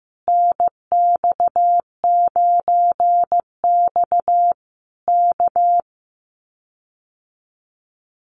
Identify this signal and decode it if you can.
Morse code, heard as a keyed tone, about 15 wpm, 700 Hz. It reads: NX9X K
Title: Common QSO Phrases Encore 15wpm